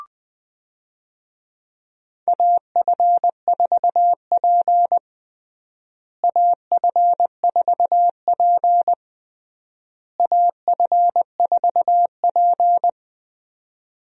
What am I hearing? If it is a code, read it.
Morse code, heard as a keyed tone, about 21 wpm, 690 Hz. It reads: AF4P AF4P AF4P
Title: USA Call Signs - 4 Characters - Repeated 3x - 20wpm